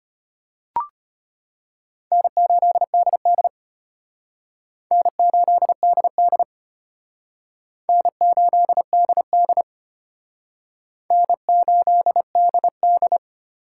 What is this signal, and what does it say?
Morse code, heard as a keyed tone, about 32 wpm, 700 Hz. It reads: N8DD N8DD N8DD N8DD